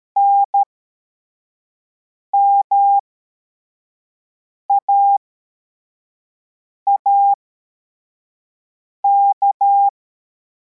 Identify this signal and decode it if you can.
Morse code, heard as a keyed tone, about 13 wpm, spaced as for 4 wpm, 800 Hz. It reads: NMAAK
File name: formation / cw / Text21